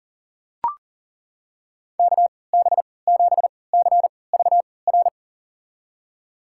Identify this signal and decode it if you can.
Morse code, heard as a keyed tone, about 40 wpm, spaced as for 23 wpm, 700 Hz. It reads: KB7CVR
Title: USA Call Signs - 3x Character Spacing - 40wpm